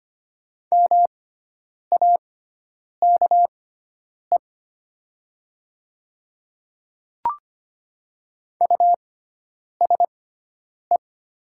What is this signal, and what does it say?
Morse code, heard as a keyed tone, about 25 wpm, spaced as for 9 wpm, 700 Hz. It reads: MAKE USE
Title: 6x Character Spacing - Top 100 Words - 25wpm